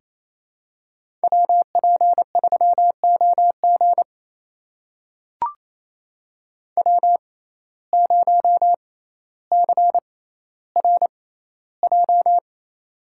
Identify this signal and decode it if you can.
Morse code, heard as a keyed tone, about 28 wpm, spaced as for 14 wpm, 700 Hz. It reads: WP3OG W0CRJ
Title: USA Call Signs - 5 Characters - 6x Character Spacing - 28wpm